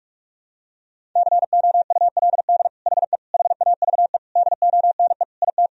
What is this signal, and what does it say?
Morse code, heard as a keyed tone, about 46 wpm, 700 Hz. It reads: COULD HE HAVE DONE IT
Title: Sentences from Top 1000 Words - Review Part 3 - 45wpm